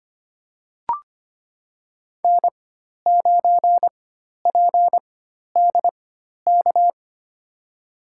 Morse code, heard as a keyed tone, about 25 wpm, spaced as for 12 wpm, 700 Hz. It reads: N9PDK
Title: USA Call Signs - 5 Characters - 4x Character Spacing - 25wpm